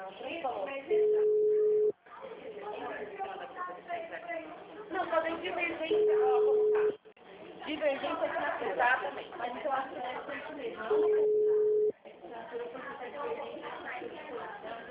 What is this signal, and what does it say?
Ringback tone (Europe, 425 Hz), with speech.
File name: out- -2004